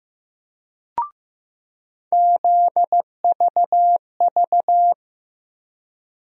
Morse code, heard as a keyed tone, about 15 wpm, 700 Hz. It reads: ZVV
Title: USA Call Sign Postfixes - 3 Letters 15wpm